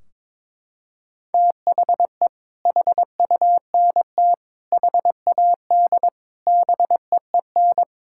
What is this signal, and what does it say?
Morse code, heard as a keyed tone, about 22 wpm, 700 Hz. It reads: THE HUNT HAD BEEN
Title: Sentences from Top 1600 Words 22wpm